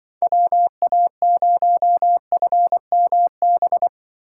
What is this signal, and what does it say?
Morse code, heard as a keyed tone, about 25 wpm, 700 Hz. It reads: WA0FMB
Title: Speed Racing - USA 2x3 Call Signs 20wpm